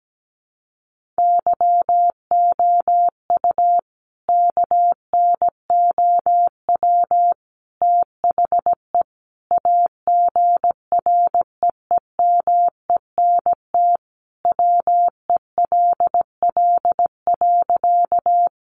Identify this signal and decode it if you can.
Morse code, heard as a keyed tone, about 17 wpm, 700 Hz. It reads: YOU KNOW THE AGREEMENT WELL.
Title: Sentences from Top 1500 Words 17wpm